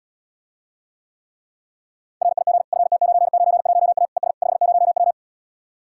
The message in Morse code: RIG FT991A ES 50W

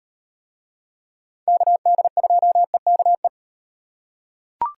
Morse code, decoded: KD2EKE

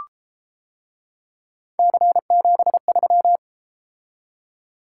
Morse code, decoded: C73

